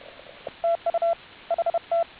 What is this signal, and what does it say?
Morse code, heard as a keyed tone, about 30 wpm, 690 Hz, with radio noise. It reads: TU HT